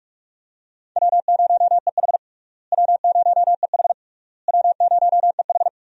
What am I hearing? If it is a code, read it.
Morse code, heard as a keyed tone, about 42 wpm, 700 Hz. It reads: W0EH W0EH W0EH
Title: USA Call Signs - 4 Characters - Repeated 3x - 45wpm